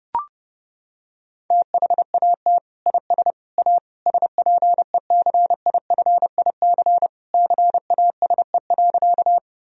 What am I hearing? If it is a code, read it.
Morse code, heard as a keyed tone, about 32 wpm, 700 Hz. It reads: THAT IS A SPECIFIC CASE.
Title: Sentences from Top 1000 Words - Review Part 3 - 30wpm